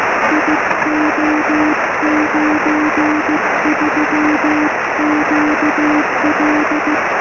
Morse code, heard as a keyed tone, about 15 wpm, 310 Hz, with noise. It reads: IO93QL
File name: sm7lcb / g4cch 23cm